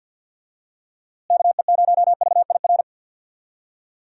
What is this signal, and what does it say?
Morse code, heard as a keyed tone, about 48 wpm, 700 Hz. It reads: KE0VIR
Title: USA Call Signs - 6 Characters - Repeated 3x - 50wpm